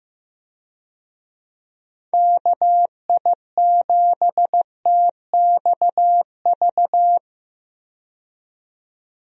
Morse code, heard as a keyed tone, about 15 wpm, 700 Hz. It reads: KI7TXV
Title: USA Call Signs - 6 Characters - 15wpm